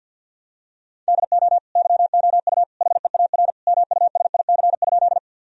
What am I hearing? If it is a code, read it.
Morse code, heard as a keyed tone, about 50 wpm, 700 Hz. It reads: DO YOU HEAR MUSIC?
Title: Sentences from Top 1000 Words - Review Part 1 - 50wpm